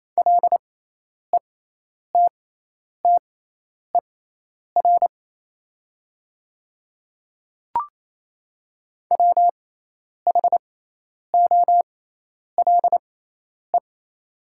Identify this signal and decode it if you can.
Morse code, heard as a keyed tone, about 28 wpm, spaced as for 10 wpm, 700 Hz. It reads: LETTER WHOLE
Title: 6x Character Spacing - Top 500 Words - 28wpm